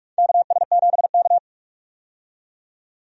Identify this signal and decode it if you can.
Morse code, heard as a keyed tone, about 46 wpm, 700 Hz. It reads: KS7K
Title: USA Call Signs - 4 Characters - 45wpm